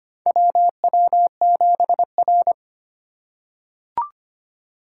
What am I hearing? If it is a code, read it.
Morse code, heard as a keyed tone, about 25 wpm, 700 Hz. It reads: WW7R
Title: USA Call Signs - 4 Characters - 4x Character Spacing - 25wpm